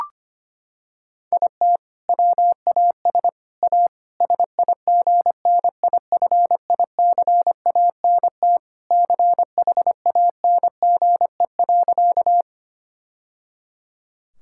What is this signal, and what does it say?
Morse code, heard as a keyed tone, about 25 wpm, 700 Hz. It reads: IT WAS A SIGNIFICANT CHANGE.
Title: Sentences from Top 1000 Words - Review Part 6 - 25wpm